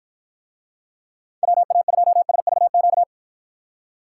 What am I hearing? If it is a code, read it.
Morse code, heard as a keyed tone, about 53 wpm, 700 Hz. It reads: WA2SVQ